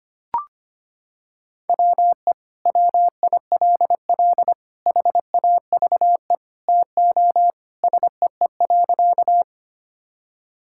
Morse code, heard as a keyed tone, about 25 wpm, 700 Hz. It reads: WE WILL HAVE TO SEE.
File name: practice / 25wpm / Sentences with 3-Word Phrases - 25wpm